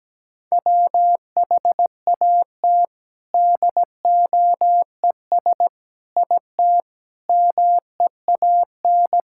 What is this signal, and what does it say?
Morse code, heard as a keyed tone, about 17 wpm, 700 Hz. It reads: WHAT DOES IT MEAN
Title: Sentences from Top 1000 Words - Review Part 6 - 17wpm